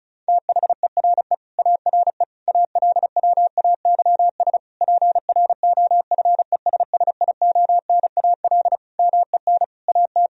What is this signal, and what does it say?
Morse code, heard as a keyed tone, about 36 wpm, 700 Hz. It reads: THERE ARE ALWAYS PROFESSIONAL MEN AT
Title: Sentences from Top 1000 Words - Review Part 4 - 35wpm